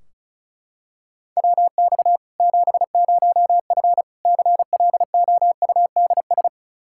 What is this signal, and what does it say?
Morse code, heard as a keyed tone, about 36 wpm, 700 Hz. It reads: WX 70F CLOUDS
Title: Practice QSO - Exchange 2 - Info - 35wpm